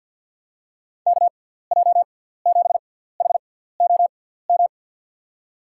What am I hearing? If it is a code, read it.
Morse code, heard as a keyed tone, about 53 wpm, spaced as for 17 wpm, 700 Hz. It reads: KJ7HOM
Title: USA Call Signs - 6 Characters - 6x Character Spacing - 50wpm